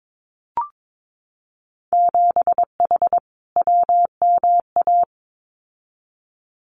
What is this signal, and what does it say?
Morse code, heard as a keyed tone, about 22 wpm, 700 Hz. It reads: 7H WMA
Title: Winter Field Day Exchanges Encore 22wpm